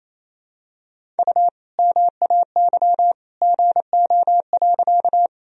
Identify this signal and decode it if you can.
Morse code, heard as a keyed tone, about 28 wpm, 700 Hz. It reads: U MAY GO.
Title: Sentences from Top 100 Words and Abbreviations 28wpm